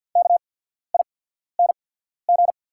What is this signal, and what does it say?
Morse code, heard as a keyed tone, about 48 wpm, spaced as for 14 wpm, 690 Hz. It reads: KING
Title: 8x Character Spacing - Top 500 Words - 50wpm